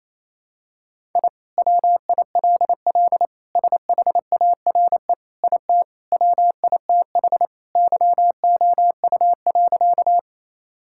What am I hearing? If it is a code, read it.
Morse code, heard as a keyed tone, about 28 wpm, 700 Hz. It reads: I WILL SHARE IT WITH YOU.
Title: Sentences from Top 1000 Words - Review Part 6 - 28wpm